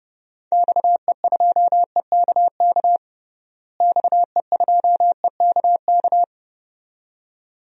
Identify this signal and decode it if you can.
Morse code, heard as a keyed tone, about 31 wpm, 700 Hz. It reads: XE2EKK XE2EKK
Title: DXCC-100 Call Signs - 6 Characters - Repeated 2x - 30wpm